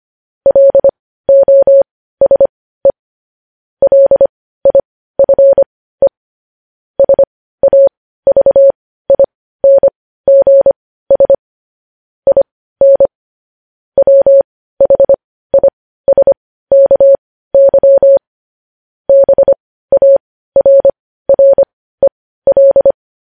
Morse code, heard as a keyed tone, about 25 wpm, spaced as for 15 wpm, 550 Hz. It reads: LOSE LIFE SAVINGS IN WHISKY BARREL
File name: podcast / News headlines-1x-25x15